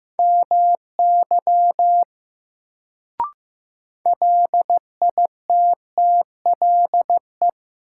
Morse code, heard as a keyed tone, about 15 wpm, 700 Hz. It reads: MY LITTLE
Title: Sentences from Top 200 Words - New Word Drill - 15wpm